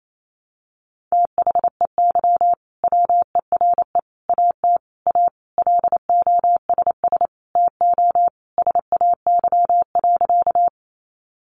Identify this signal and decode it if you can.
Morse code, heard as a keyed tone, about 28 wpm, 700 Hz. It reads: THEY WERE AT A LOSS TO SAY.
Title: Sentences from Top 1400 Words 28wpm